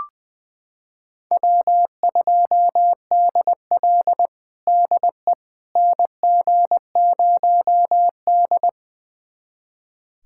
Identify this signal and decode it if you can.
Morse code, heard as a keyed tone, about 20 wpm, 700 Hz. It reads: W2DL DE NG0D
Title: USA Callsign DE Callsign - 4 Characters - 20wpm